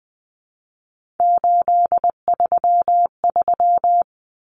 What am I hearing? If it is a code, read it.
Morse code, heard as a keyed tone, about 20 wpm, 700 Hz. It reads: 833
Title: Numbers - 3-Digit 20wpm